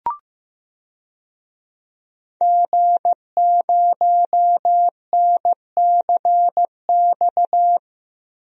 Morse code, heard as a keyed tone, about 15 wpm, 700 Hz. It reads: G0NCX
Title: DXCC-100 Call Signs - 5 Characters - Repeated 3x - 15wpm